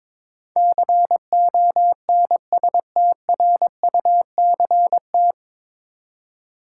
Morse code, heard as a keyed tone, about 22 wpm, 700 Hz. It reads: CONSTRUCT
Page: Morse Code Ninja practice 22wpm